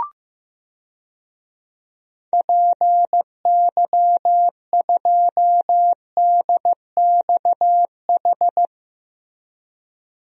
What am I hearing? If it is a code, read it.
Morse code, heard as a keyed tone, about 15 wpm, 700 Hz. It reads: PY2DXH
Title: DXCC-100 Call Signs - 6 Characters - Repeated 3x - 15wpm